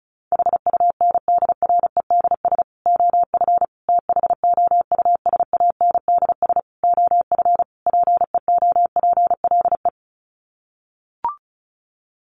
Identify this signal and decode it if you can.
Morse code, heard as a keyed tone, about 34 wpm, 700 Hz. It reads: HUNDREDS OF THOUSANDS OF PEOPLE
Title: Sets of 5 Words Encore2 35wpm